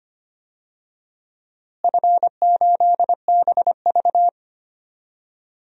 Morse code, heard as a keyed tone, about 25 wpm, 700 Hz. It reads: F8BV
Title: DXCC-100 Call Signs - 4 Characters - Repeated 3x - 25wpm